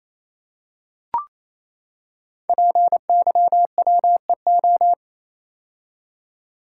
Morse code, heard as a keyed tone, about 28 wpm, 700 Hz. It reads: PYWEO
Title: Code Groups - 5 Letters - 28wpm